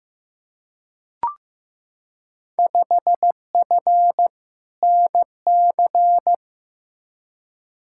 Morse code, heard as a keyed tone, about 15 wpm, 700 Hz. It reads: 5F NC
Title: Field Day Exchanges Encore - 15wpm